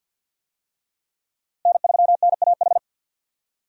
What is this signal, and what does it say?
Morse code, heard as a keyed tone, about 48 wpm, 700 Hz. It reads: N3NAH
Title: USA Call Signs - 5 Characters - Repeated 2x - 50wpm